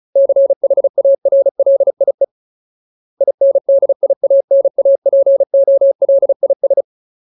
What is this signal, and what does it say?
Morse code, heard as a keyed tone, about 34 wpm, 550 Hz. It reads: CHARLIE INDIANAPOLIS
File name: Ultimate Word List / 35WPM Ultimate